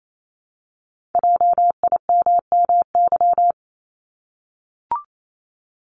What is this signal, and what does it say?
Morse code, heard as a keyed tone, about 28 wpm, 700 Hz. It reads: JIMMY
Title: Multi-letter Phonemes Repeated 3x 28wpm